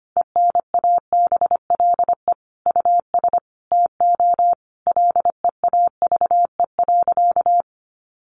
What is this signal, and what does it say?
Morse code, heard as a keyed tone, about 25 wpm, 700 Hz. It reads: ENABLE US TO LEAVE.
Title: Sentences from Top 1500 Words 25wpm